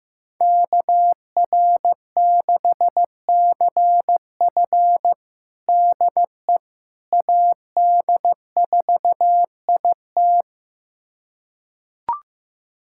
Morse code, heard as a keyed tone, about 16 wpm, 700 Hz. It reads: KR6CF DE AD4IT
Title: USA Callsign DE Callsign - 5 Characters - 15wpm